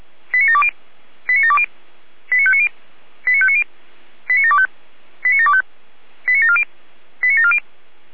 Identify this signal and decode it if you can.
Five-tone selective call (ZVEI), heard as quick runs of five tones, tones of about 70 ms. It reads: 89720 89720 89590 89590 89725 89725 89740 89740
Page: /455/SDR/ZVEI/89730/